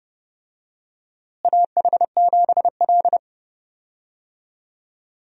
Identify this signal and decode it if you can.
Morse code, heard as a keyed tone, about 30 wpm, 700 Hz. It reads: AH7L
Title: USA Call Signs - 4 Characters - Repeated 3x - 30wpm